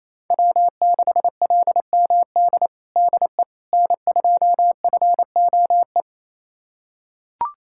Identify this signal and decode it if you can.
Morse code, heard as a keyed tone, about 28 wpm, 700 Hz. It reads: W6LMD DE N2FOE
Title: USA Callsign DE Callsign - 5 Characters - 28wpm